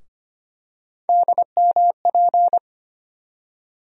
Morse code, heard as a keyed tone, about 25 wpm, 700 Hz. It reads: DMP
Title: USA Call Sign Postfixes - 3 Letters 25wpm